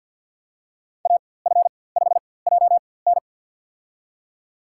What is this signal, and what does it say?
Morse code, heard as a keyed tone, about 52 wpm, spaced as for 24 wpm, 690 Hz. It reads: AF5JN
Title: USA Call Signs - 4x Character Spacing - 50wpm